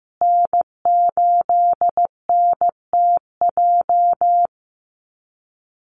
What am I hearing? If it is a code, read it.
Morse code, heard as a keyed tone, about 15 wpm, 700 Hz. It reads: N8NTJ
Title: USA Call Signs Repeated 3x 15wpm